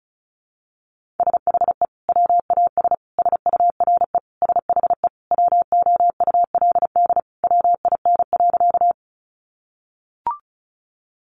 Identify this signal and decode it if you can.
Morse code, heard as a keyed tone, about 34 wpm, 700 Hz. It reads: SHE WAS SURE SHE WOULD WIN.
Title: Sentences from Top 1500 Words 35wpm